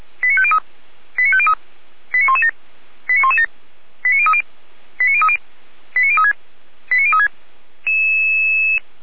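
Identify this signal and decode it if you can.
Five-tone selective call (ZVEI), heard as quick runs of five tones, tones of about 70 ms. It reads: 89503 89503 89117 89117 89030 89030 89036 89036
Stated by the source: ZVEI five-tone selective call received by SDR.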